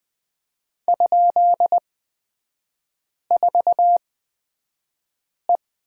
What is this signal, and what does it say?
Morse code, heard as a keyed tone, about 20 wpm, spaced as for 5 wpm, 700 Hz. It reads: ?4E
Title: 224 Lesson 36e - BT - Three Characters 20wpm